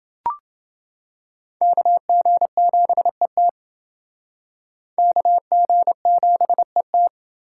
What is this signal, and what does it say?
Morse code, heard as a keyed tone, about 28 wpm, 700 Hz. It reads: KG7ET KG7ET